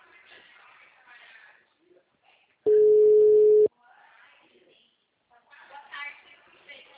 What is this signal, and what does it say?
Ringback tone (Europe, 425 Hz), with speech.